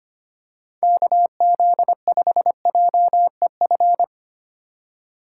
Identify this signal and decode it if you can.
Morse code, heard as a keyed tone, about 25 wpm, 700 Hz. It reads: KZ5JEF
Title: USA Call Signs - 6 Characters - 2x Character Spacing - 25wpm